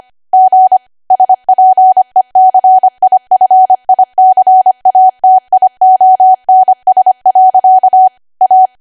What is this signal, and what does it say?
Morse code, heard as a keyed tone, about 25 wpm, 750 Hz. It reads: G SPECIFICATIONS. A
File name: Archive / 25 WPM / 25WPM